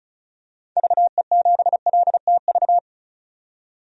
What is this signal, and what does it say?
Morse code, heard as a keyed tone, about 34 wpm, 700 Hz. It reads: VE7LTV
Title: DXCC-100 Call Signs - 6 Characters - 35wpm